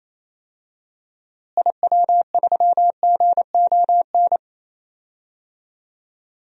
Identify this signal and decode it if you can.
Morse code, heard as a keyed tone, about 28 wpm, 700 Hz. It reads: IW3GON
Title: DXCC-100 Call Signs - 6 Characters - Repeated 3x - 28wpm